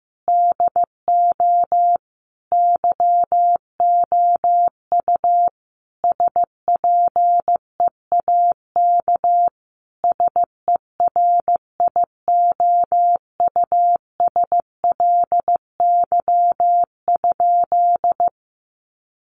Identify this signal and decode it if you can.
Morse code, heard as a keyed tone, about 15 wpm, 700 Hz. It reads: DO YOU SPEAK SERIOUSLY?
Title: Sentences from Top 1200 Words 15wpm